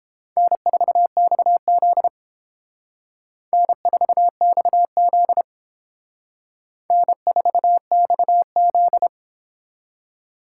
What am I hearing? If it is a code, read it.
Morse code, heard as a keyed tone, about 28 wpm, 700 Hz. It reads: N4XZ N4XZ N4XZ